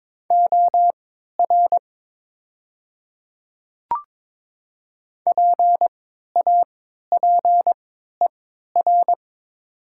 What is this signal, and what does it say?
Morse code, heard as a keyed tone, about 22 wpm, spaced as for 13 wpm, 700 Hz. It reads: OR PAPER
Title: 3x Character Spacing - Top 300 Words - 22wpm